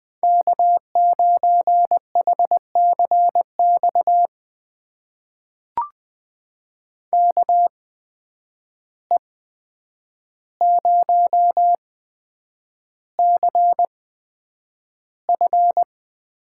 Morse code, heard as a keyed tone, about 20 wpm, spaced as for 9 wpm, 700 Hz. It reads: K9HCX KE0CF
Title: USA Call Signs - 8x Character Spacing - 20wpm